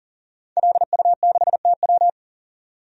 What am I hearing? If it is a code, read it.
Morse code, heard as a keyed tone, about 39 wpm, 700 Hz. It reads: LU6TW
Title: DXCC-100 Call Signs - 5 Characters - 40wpm